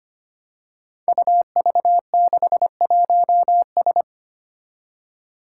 Morse code, heard as a keyed tone, about 25 wpm, 700 Hz. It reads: UV61S
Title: Five Challenging Letter-Number Combinations 25wpm